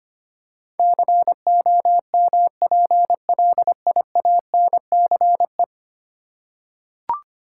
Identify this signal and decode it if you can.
Morse code, heard as a keyed tone, about 25 wpm, 700 Hz. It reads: COMPLIANCE
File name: 10 Letter Words - First 3 Letters Given - 25wpm